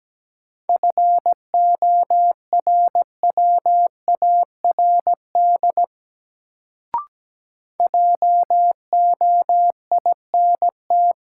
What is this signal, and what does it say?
Morse code, heard as a keyed tone, about 17 wpm, 700 Hz. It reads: FORWARD JOINT
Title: Sentences from Top 1300 Words - New Word Drill Alt - 17wpm